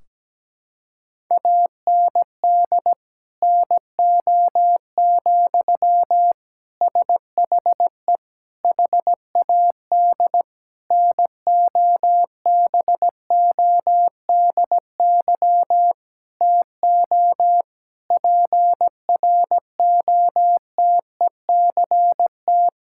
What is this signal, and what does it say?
Morse code, heard as a keyed tone, about 17 wpm, 700 Hz. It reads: AND NO, SHE HAD NOBODY TO PROTECT
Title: Sentences from Top 1600 Words 17wpm